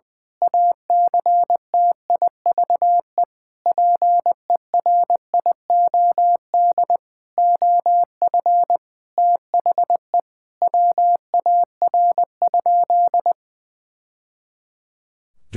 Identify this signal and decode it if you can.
Morse code, heard as a keyed tone, about 20 wpm, 700 Hz. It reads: ACTIVE PERIOD OF THE WAR?
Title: Sentences from Top 1000 Words - Review Part 6 - 20wpm